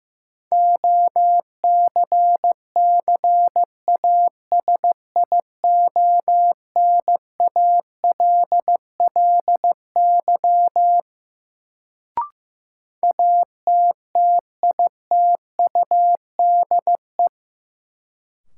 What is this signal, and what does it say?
Morse code, heard as a keyed tone, about 15 wpm, 700 Hz. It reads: OCCASIONALLY ATTITUDE
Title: Sentences from Top 1300 Words - New Word Drill - 15wpm